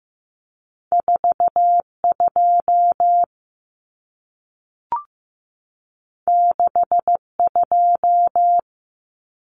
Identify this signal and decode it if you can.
Morse code, heard as a keyed tone, about 15 wpm, 700 Hz. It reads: 42 62
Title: Common Cut Numbers - 2-Digit 15wpm